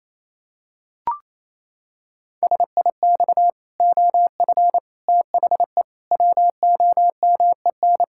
Morse code, heard as a keyed tone, about 28 wpm, 700 Hz. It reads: SIX OF THE WOMEN
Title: Pangram Challenge v2 - 28wpm